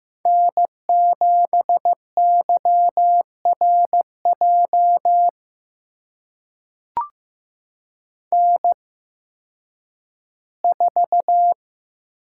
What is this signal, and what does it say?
Morse code, heard as a keyed tone, about 15 wpm, spaced as for 10 wpm, 700 Hz. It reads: N7YRJ N4
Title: USA Call Signs - 5 Characters - 8x Character Spacing - 15wpm